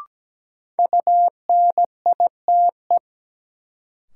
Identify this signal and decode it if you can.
Morse code, heard as a keyed tone, about 17 wpm, 700 Hz. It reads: UNITE
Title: Sentences from Top 1000 Words - Review Part 5 - New Word Drill - 17wpm